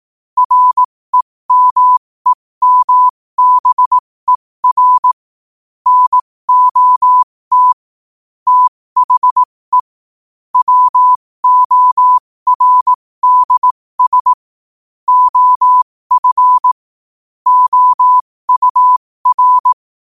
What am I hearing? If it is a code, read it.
Morse code, heard as a keyed tone, about 18 wpm, spaced as for 15 wpm, 1000 Hz. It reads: REMEMBER NOT THE WORDS OF OUR
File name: qotdfiles / 15wpm